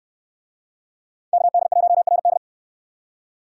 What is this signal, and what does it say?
Morse code, heard as a keyed tone, about 62 wpm, 700 Hz. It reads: KD1RD